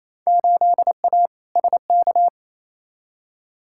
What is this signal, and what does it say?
Morse code, heard as a keyed tone, about 28 wpm, 700 Hz. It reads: 8A SK
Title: Field Day Exchanges Encore - 28wpm